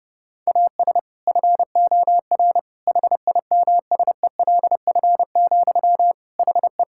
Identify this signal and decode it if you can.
Morse code, heard as a keyed tone, about 30 wpm, 700 Hz. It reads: AS FOR HIMSELF, HE
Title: Sentences from Top 1600 Words 30wpm